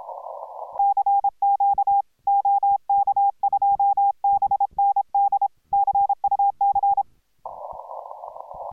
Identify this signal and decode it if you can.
Morse code, heard as a keyed tone, about 26 wpm, 810 Hz. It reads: CQ OK2BND CUC